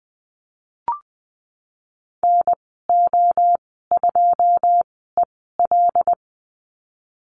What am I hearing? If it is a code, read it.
Morse code, heard as a keyed tone, about 19 wpm, spaced as for 14 wpm, 700 Hz. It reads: NO2EL
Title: 088 Lesson 15g - Number 5 - Call Signs 20wpm